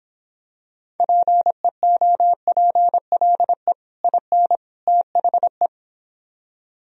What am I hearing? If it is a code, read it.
Morse code, heard as a keyed tone, about 26 wpm, 700 Hz. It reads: PEOPLE IN THE